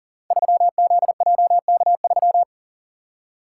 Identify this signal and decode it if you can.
Morse code, heard as a keyed tone, about 39 wpm, 700 Hz. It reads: 3ZJK3
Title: Code Groups - 5 Letter:Numbers - 40wpm